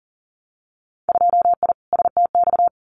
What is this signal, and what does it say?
Morse code, heard as a keyed tone, about 40 wpm, 700 Hz. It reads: 2I STX
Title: Winter Field Day Exchanges 40wpm